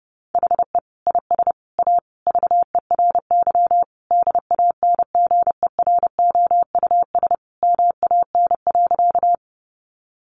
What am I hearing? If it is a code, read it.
Morse code, heard as a keyed tone, about 30 wpm, 700 Hz. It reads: HE IS A VERY DANGEROUS MAN.
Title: Sentences from Top 1200 Words 30wpm